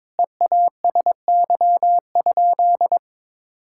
Morse code, heard as a keyed tone, about 22 wpm, 700 Hz. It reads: EASY?
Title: Sentences from Top 1000 Words - Review Part 5 - 22wpm